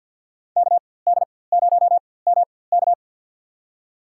Morse code, heard as a keyed tone, about 52 wpm, spaced as for 23 wpm, 700 Hz. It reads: KD0MK
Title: USA Call Signs - 5 Characters - 4x Character Spacing - 50wpm